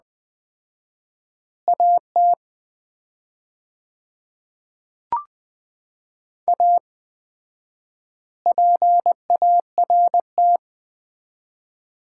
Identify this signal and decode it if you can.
Morse code, heard as a keyed tone, about 20 wpm, 700 Hz. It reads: AT A PART
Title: 4x Word-Spacing - Sets of 3 Words - 20wpm